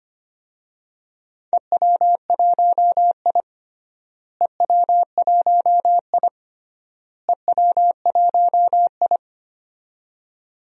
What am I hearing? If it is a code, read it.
Morse code, heard as a keyed tone, about 25 wpm, 700 Hz. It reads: EW1I EW1I EW1I
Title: DXCC-100 Call Signs - 4 Characters - Repeated 3x - 25wpm